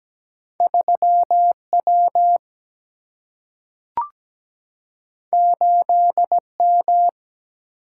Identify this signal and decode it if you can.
Morse code, heard as a keyed tone, about 17 wpm, 700 Hz. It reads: 3 W 8 M
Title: Single-Letter Single-Number 17wpm